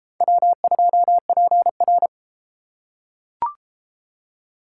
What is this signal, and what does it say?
Morse code, heard as a keyed tone, about 30 wpm, 700 Hz. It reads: W2PR